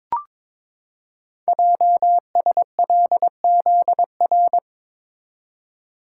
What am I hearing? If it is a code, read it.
Morse code, heard as a keyed tone, about 22 wpm, 700 Hz. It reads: JSLZR
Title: Code Groups - 5 Letters - 22wpm